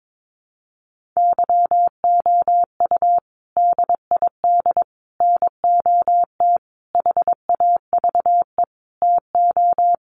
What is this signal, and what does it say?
Morse code, heard as a keyed tone, about 22 wpm, 700 Hz. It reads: YOU DID NOT HAVE TO
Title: Sets of 5 Words Encore 22wpm